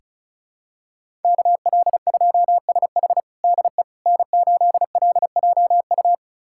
Morse code, heard as a keyed tone, about 36 wpm, 700 Hz. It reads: KL2SH DE N8LJU
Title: USA Callsign DE Callsign 35wpm